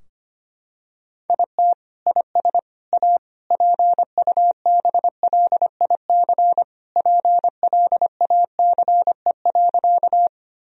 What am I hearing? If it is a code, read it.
Morse code, heard as a keyed tone, about 25 wpm, 700 Hz. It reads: IT IS A PUBLIC PLACE.
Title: Sentences from Top 1000 Words - Review Part 3 - 25wpm